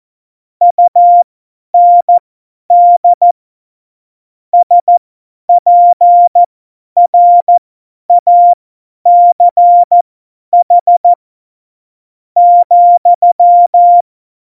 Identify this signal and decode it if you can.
Morse code, heard as a keyed tone, about 14 wpm, spaced as for 9 wpm, 700 Hz. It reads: UND SPRACH,
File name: Morsekurs / Witze / Witz 18. Februar 2020, 50 F70